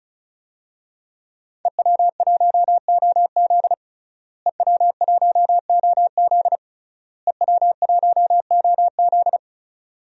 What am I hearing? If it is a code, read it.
Morse code, heard as a keyed tone, about 34 wpm, 700 Hz. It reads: EW1OZ EW1OZ EW1OZ
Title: DXCC-100 Call Signs - 5 Characters - Repeated 3x - 35wpm